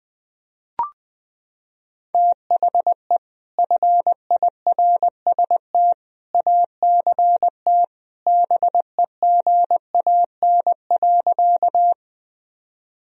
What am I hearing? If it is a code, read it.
Morse code, heard as a keyed tone, about 21 wpm, 700 Hz. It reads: THE FIRST ACT BEGAN.
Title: Sentences from Top 1000 Words - Review Part 1 - 20wpm